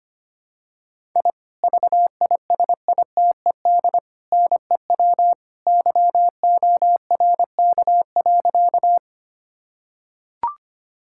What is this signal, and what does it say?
Morse code, heard as a keyed tone, about 25 wpm, 700 Hz. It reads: I VISITED NEW YORK.
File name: practice / 25wpm / Sentences with 2-Word Phrases - 25wpm